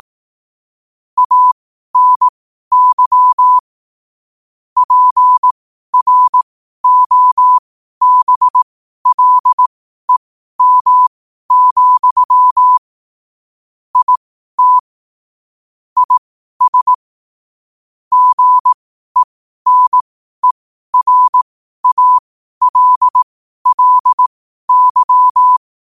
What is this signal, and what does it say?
Morse code, heard as a keyed tone, about 18 wpm, spaced as for 12 wpm, 1000 Hz. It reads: ANY PROBLEM, IT IS GENERALLY
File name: 12wpm